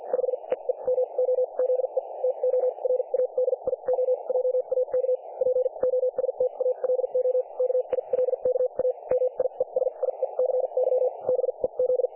Morse code, heard as a keyed tone, about 50 wpm, spaced as for 39 wpm, 500 Hz, with radio noise. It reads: 5 EE MOZE TO RADIO JAK POSTALO KILKA MIESIECY BEZ
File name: dlazaawansowanych / nagraniazpasma / websdr recording 3564.7kHz